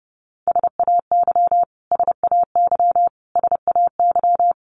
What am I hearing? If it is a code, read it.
Morse code, heard as a keyed tone, about 30 wpm, 700 Hz. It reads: SAY SAY SAY
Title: Top 500 Words Repeated 30wpm